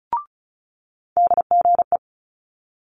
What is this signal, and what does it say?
Morse code, heard as a keyed tone, about 36 wpm, 700 Hz. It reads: DGE
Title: Multi-letter Phonemes Quick Fire 35wpm